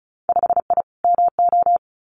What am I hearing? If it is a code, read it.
Morse code, heard as a keyed tone, about 36 wpm, 700 Hz. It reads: 5I MO
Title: Winter Field Day Exchanges Encore 35wpm